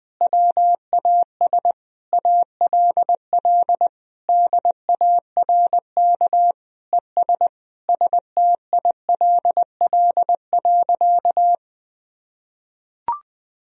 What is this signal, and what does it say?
Morse code, heard as a keyed tone, about 20 wpm, 700 Hz. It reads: WAS ALL DARK ES STILL.
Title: Sentences from Top 500 Words and Abbreviations 20wpm